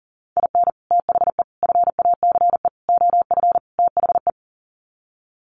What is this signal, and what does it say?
Morse code, heard as a keyed tone, about 40 wpm, 700 Hz. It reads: IN THE FACE OF THE
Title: Sets of 5 Words 40wpm